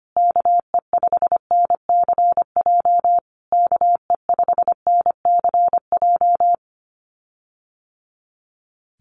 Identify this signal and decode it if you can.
Morse code, heard as a keyed tone, about 25 wpm, 700 Hz. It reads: KE5NCJ KE5NCJ
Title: USA Call Signs Repeated 3x 25wpm